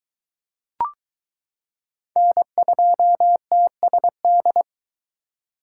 Morse code, heard as a keyed tone, about 22 wpm, 700 Hz. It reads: N2TSD